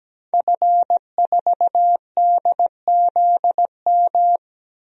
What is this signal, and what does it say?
Morse code, heard as a keyed tone, about 17 wpm, 700 Hz. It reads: F4DZM
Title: DXCC-100 Call Signs - 5 Characters - 17wpm